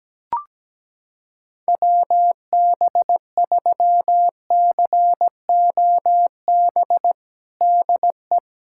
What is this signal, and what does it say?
Morse code, heard as a keyed tone, about 17 wpm, 700 Hz. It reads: WB3COB DE
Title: USA Callsign DE Callsign - 6 Characters - 17wpm